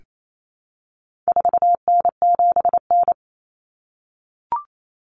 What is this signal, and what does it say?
Morse code, heard as a keyed tone, about 28 wpm, 700 Hz. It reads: 4N7N
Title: Common Cut Numbers - 4-Digit 28wpm